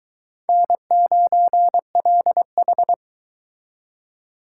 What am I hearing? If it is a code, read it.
Morse code, heard as a keyed tone, about 22 wpm, 700 Hz. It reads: N9LH